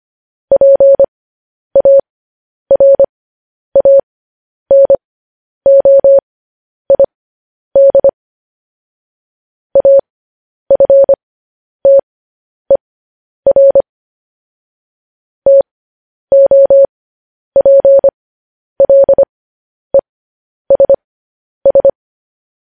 Morse code, heard as a keyed tone, about 25 wpm, spaced as for 10 wpm, 550 Hz. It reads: PARANOID AFTER TOPLESS
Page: News Headlines in Morse Code at 10 WPM